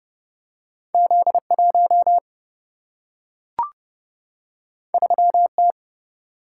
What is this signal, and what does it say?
Morse code, heard as a keyed tone, about 30 wpm, 700 Hz. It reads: Z 1 3 T
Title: Single-Letter Single-Number 30wpm